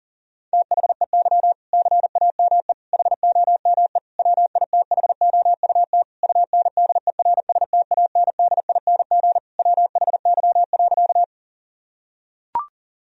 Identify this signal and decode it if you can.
Morse code, heard as a keyed tone, about 40 wpm, 700 Hz. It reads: THEY CAME HOME WITHOUT UNDERSTANDING WHY.
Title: Sentences from Top 1000 Words - Review Part 3 - 40wpm